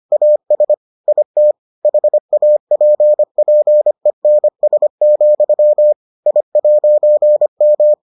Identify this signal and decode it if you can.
Morse code, heard as a keyed tone, about 25 wpm, 600 Hz. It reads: AS IT HAPPENS, I'M